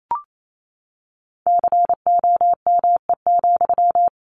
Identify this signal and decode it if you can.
Morse code, heard as a keyed tone, about 28 wpm, 700 Hz. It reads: COME,
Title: Sentences from Top 1500 Words 28wpm